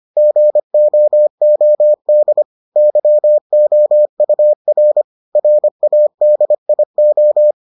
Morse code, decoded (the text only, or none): GOOD YOUR RADIO